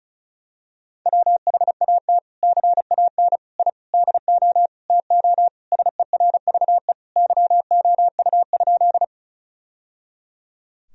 Morse code, heard as a keyed tone, about 36 wpm, 700 Hz. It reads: WHAT CAN I DO TO SERVE YOU?
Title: Sentences from Top 1000 Words - Review Part 2 - 35wpm